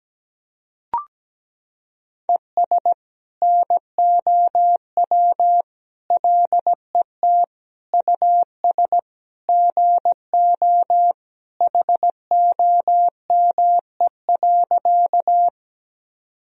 Morse code, heard as a keyed tone, about 17 wpm, 700 Hz. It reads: ES NOW LET US GO HOME.
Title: Sentences from Top 300 Words and Abbreviations 17wpm